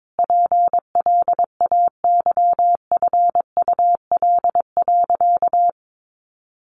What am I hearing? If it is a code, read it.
Morse code, heard as a keyed tone, about 22 wpm, 700 Hz. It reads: PLAYFUL.
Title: Easy Phrases 22wpm